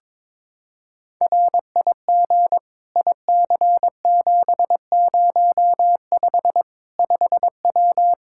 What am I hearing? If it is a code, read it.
Morse code, heard as a keyed tone, about 22 wpm, 700 Hz. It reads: RIG IC705 5W
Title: K1USN SST QSO Practice 22wpm